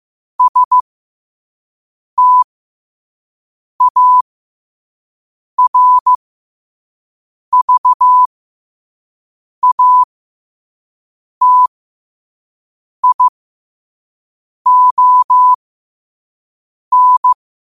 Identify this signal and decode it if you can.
Morse code, heard as a keyed tone, about 15 wpm, spaced as for 5 wpm, 1000 Hz. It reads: STARVATION
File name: qotdfiles / 5wpm